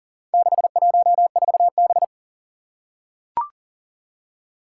Morse code, heard as a keyed tone, about 39 wpm, 700 Hz. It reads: 614B
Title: Four Challenging Letter-Number Combinations 40wpm